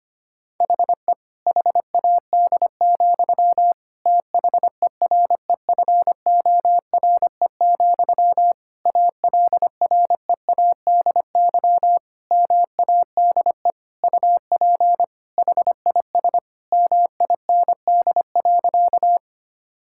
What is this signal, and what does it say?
Morse code, heard as a keyed tone, about 25 wpm, 700 Hz. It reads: HE HAD, THEREFORE, ALREADY MADE UP HIS MIND.
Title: Sentences from Top 1000 Words - Review Part 5 - 25wpm